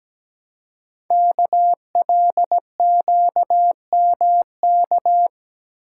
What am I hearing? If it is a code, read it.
Morse code, heard as a keyed tone, about 17 wpm, 700 Hz. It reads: KLQMK
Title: Code Groups - 5 Letters - 17wpm